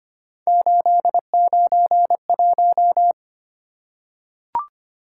Morse code, heard as a keyed tone, about 25 wpm, 700 Hz. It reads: 891
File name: Three Letter-Number Combinations - 25wpm